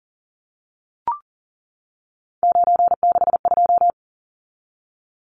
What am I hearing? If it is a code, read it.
Morse code, heard as a keyed tone, about 40 wpm, 700 Hz. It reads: 962
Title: Numbers - 3-Digit 40wpm